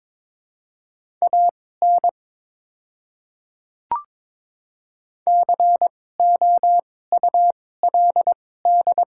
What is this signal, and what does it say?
Morse code, heard as a keyed tone, about 22 wpm, spaced as for 16 wpm, 700 Hz. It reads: AN COULD
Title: 2x Character Spacing - Top 300 Words - 22wpm